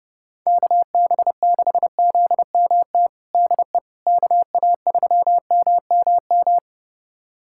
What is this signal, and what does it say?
Morse code, heard as a keyed tone, about 32 wpm, 700 Hz. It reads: KB6ZMT DE KA3MMM
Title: USA Callsign DE Callsign 30wpm